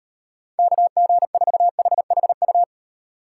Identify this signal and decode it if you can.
Morse code, heard as a keyed tone, about 37 wpm, 700 Hz. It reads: KG4HHU